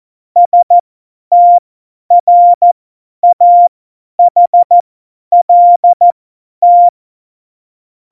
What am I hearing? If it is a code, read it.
Morse code, heard as a keyed tone, about 14 wpm, spaced as for 10 wpm, 700 Hz. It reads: STRAHLT
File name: Morsekurs / Witze / Witz 08. Oktober 2019, 50 F70